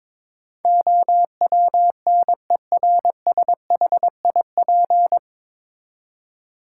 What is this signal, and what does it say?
Morse code, heard as a keyed tone, about 22 wpm, 700 Hz. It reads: OWNERSHIP
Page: Morse Code Ninja practice 22wpm